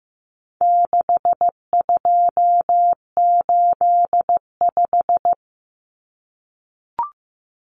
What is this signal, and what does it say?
Morse code, heard as a keyed tone, about 15 wpm, 700 Hz. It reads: 6285
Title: Common Cut Numbers - 4-Digit 15wpm